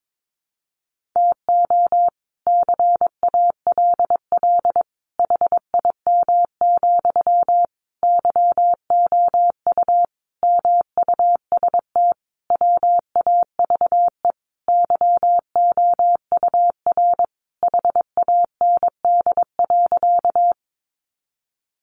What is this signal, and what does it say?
Morse code, heard as a keyed tone, about 22 wpm, 700 Hz. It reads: TO CALL HIM, YOU MUST WAVE YOUR HAND.
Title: Sentences from Top 1400 Words 22wpm